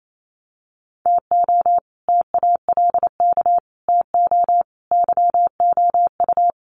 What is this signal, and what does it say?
Morse code, heard as a keyed tone, about 27 wpm, 700 Hz. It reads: TO TALK TO YOU
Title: Sets of 5 Words Encore 28wpm